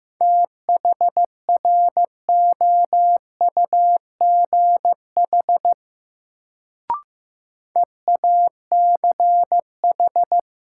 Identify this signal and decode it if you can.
Morse code, heard as a keyed tone, about 15 wpm, 700 Hz. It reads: THROUGH EACH
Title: Sets of 2 Words - New Word Drill - 15wpm